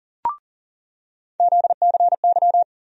Morse code, heard as a keyed tone, about 40 wpm, 700 Hz. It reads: ZCY
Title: USA Call Sign Postfixes - 3 Letters 40wpm